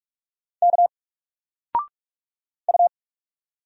Morse code, heard as a keyed tone, about 45 wpm, 700 Hz. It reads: K U
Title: Single Letter-Number 45wpm